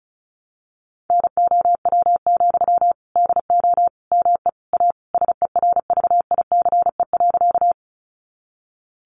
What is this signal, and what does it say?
Morse code, heard as a keyed tone, about 34 wpm, 700 Hz. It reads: NOW, DO ME A SERVICE.
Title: Sentences from Top 700 Words 35wpm